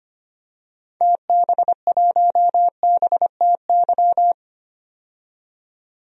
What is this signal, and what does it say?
Morse code, heard as a keyed tone, about 25 wpm, 700 Hz. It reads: TB1BTY
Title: DXCC-100 Call Signs - 6 Characters - 25wpm